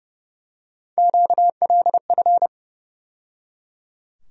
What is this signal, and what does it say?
Morse code, heard as a keyed tone, about 30 wpm, 700 Hz. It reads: QLF
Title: USA Call Sign Postfixes - 3 Letters 30wpm